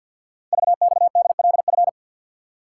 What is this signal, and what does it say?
Morse code, heard as a keyed tone, about 46 wpm, 700 Hz. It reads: VXDLF
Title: Code Groups - 5 Letters - 50wpm